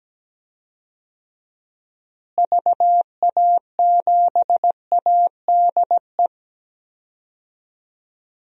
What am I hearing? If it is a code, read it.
Morse code, heard as a keyed tone, about 17 wpm, 700 Hz. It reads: VA7ADE
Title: DXCC-100 Call Signs - 6 Characters - Repeated 2x - 17wpm